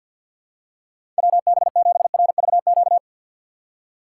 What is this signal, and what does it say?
Morse code, heard as a keyed tone, about 53 wpm, 700 Hz. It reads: WB7RVQ
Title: USA Call Signs - 6 Characters - 6x Character Spacing - 50wpm